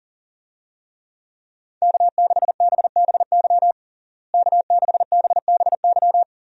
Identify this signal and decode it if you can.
Morse code, heard as a keyed tone, about 40 wpm, 700 Hz. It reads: K6BBY K6BBY
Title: USA Call Signs - 5 Characters - Repeated 2x - 40wpm